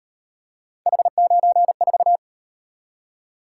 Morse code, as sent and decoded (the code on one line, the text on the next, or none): .... ----. ....-
H94